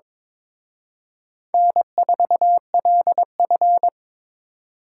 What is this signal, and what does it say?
Morse code, heard as a keyed tone, about 25 wpm, 700 Hz. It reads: N4LF
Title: Speed Racing - USA 1x2 Call Signs 22wpm